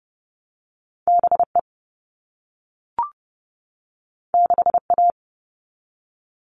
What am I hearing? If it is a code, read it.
Morse code, heard as a keyed tone, about 30 wpm, 700 Hz. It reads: BE 6A
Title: All Cut Numbers - 2-Digit 30wpm